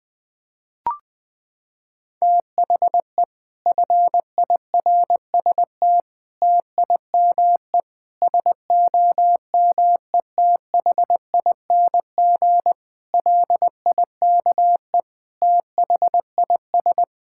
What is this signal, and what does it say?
Morse code, heard as a keyed tone, about 20 wpm, 700 Hz. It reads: THE FIRST TIME SOMETHING LIKE THIS
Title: Sets of 8 Words - 20wpm